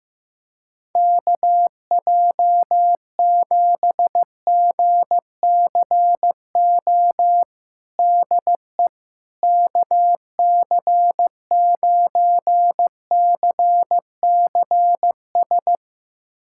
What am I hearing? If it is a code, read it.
Morse code, heard as a keyed tone, about 15 wpm, 700 Hz. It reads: KJ7GCO DE KC9CCS
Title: USA Callsign DE Callsign 15wpm